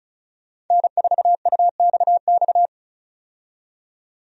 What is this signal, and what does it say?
Morse code, heard as a keyed tone, about 36 wpm, 700 Hz. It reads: N4UXX
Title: USA Call Signs - 5 Characters - 35wpm